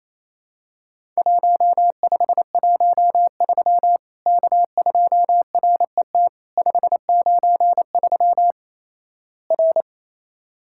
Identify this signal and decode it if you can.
Morse code, heard as a keyed tone, about 28 wpm, 700 Hz. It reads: 1513 K2RET 593 R
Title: Worked All Europe DX Contest - QTC Exchange - 28wpm